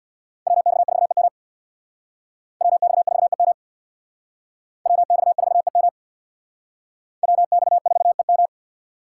Morse code, heard as a keyed tone, about 61 wpm, 700 Hz. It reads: WX4EM WX4EM WX4EM WX4EM
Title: Speed Racing - USA 2x2 Call Signs 50wpm